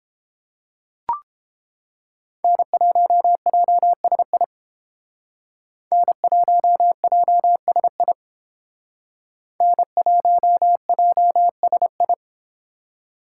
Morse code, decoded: N1JSI N1JSI N1JSI

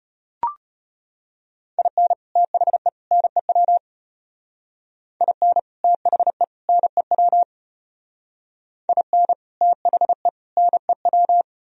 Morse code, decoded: IN THE NEW IN THE NEW IN THE NEW

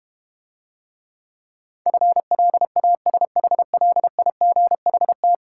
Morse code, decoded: FLASHLIGHT

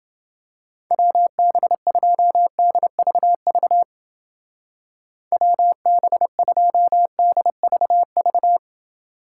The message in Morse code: WB2DVV WB2DVV

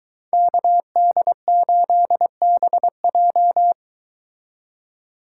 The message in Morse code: KD8BJ

